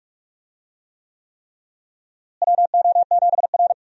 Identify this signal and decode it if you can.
Morse code, heard as a keyed tone, about 46 wpm, 700 Hz. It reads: WO7R
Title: USA Call Signs - 4 Characters - Repeated 3x - 45wpm